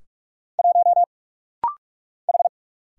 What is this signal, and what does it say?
Morse code, heard as a keyed tone, about 46 wpm, 700 Hz. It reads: 1 H
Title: Single Letter-Number Alt - Rapid-Fire 45wpm